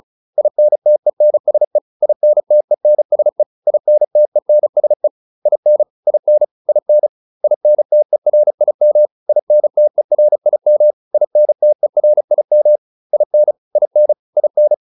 Morse code, heard as a keyed tone, about 34 wpm, 600 Hz. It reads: INTENSE INTENSE INTENSE IN IN IN INTERIM INTERIM INTERIM IN IN IN